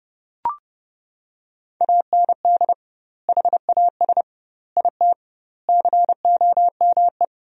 Morse code, heard as a keyed tone, about 30 wpm, 700 Hz. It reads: AND HAS IT COME
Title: 2x Word-Spacing - Sentences from Top 100 Words - 30wpm